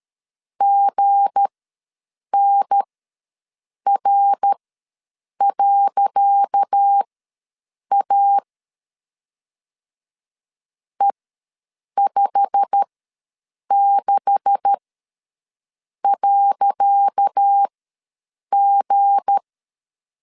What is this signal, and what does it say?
Morse code, heard as a keyed tone, about 13 wpm, spaced as for 7 wpm, 800 Hz. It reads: GNR.A E56.G
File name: LEZIONI / 10 SIGNS / 10C DOT ws 12wpm 3548